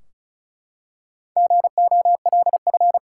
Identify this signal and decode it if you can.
Morse code, heard as a keyed tone, about 36 wpm, 700 Hz. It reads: GOLF
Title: Four Letter Words - 35wpm